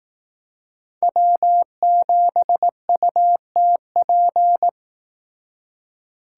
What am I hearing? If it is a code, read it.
Morse code, heard as a keyed tone, about 18 wpm, 700 Hz. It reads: W7UTP
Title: Speed Racing - USA 1x3 Call Signs 15wpm